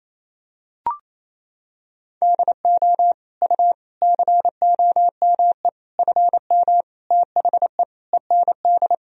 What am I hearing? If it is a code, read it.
Morse code, heard as a keyed tone, about 28 wpm, 700 Hz. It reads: DO U COME FM THE END
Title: Sentences from Top 200 Words and Abbreviations 28wpm